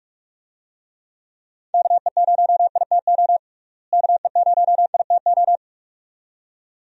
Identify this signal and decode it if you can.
Morse code, heard as a keyed tone, about 47 wpm, 700 Hz. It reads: KE0ITO KE0ITO
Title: USA Call Signs - 6 Characters - Repeated 2x - 45wpm